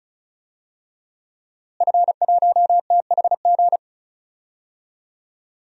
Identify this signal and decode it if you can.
Morse code, heard as a keyed tone, about 34 wpm, 700 Hz. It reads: F1THG
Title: DXCC-100 Call Signs - 5 Characters - Repeated 3x - 35wpm